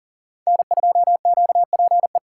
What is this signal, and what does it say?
Morse code, heard as a keyed tone, about 49 wpm, 700 Hz. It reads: N2QPE
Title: Speed Racing - USA Call Signs 40wpm